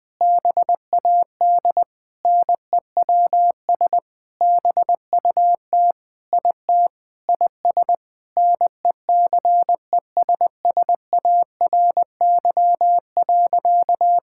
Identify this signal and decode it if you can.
Morse code, heard as a keyed tone, about 20 wpm, 700 Hz. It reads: BAD NEWS BUT IT IS NECESSARY.
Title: Sentences with 9-Word Phrases - 20wpm